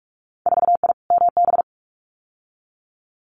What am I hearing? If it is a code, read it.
Morse code, heard as a keyed tone, about 46 wpm, 700 Hz. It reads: 4I MB
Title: Winter Field Day Exchanges Encore 45wpm